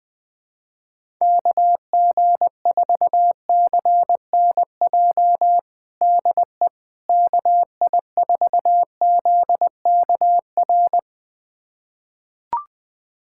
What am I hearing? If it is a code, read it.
Morse code, heard as a keyed tone, about 20 wpm, 700 Hz. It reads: KG4CNJ DE KI4ZKR
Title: USA Callsign DE Callsign - 6 Characters - 20wpm